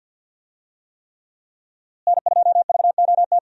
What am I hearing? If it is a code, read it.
Morse code, heard as a keyed tone, about 50 wpm, 700 Hz. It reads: N2VOT
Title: USA Call Signs - 5 Characters - Repeated 3x - 50wpm